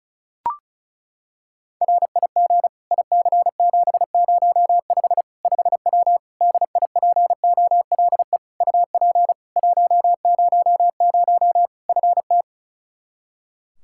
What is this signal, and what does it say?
Morse code, heard as a keyed tone, about 34 wpm, 700 Hz. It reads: RIG IC705 5W DIPOLE UP 100 FT
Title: K1USN SST QSO Practice 35wpm